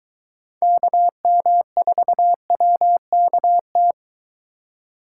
Morse code, heard as a keyed tone, about 22 wpm, 700 Hz. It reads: KM4WKT